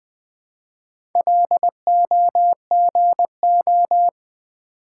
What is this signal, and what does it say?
Morse code, heard as a keyed tone, about 20 wpm, 700 Hz. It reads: LOGO
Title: Build A Word - 1 Letter At A Time - 3 to 5-Letter Words - 20wpm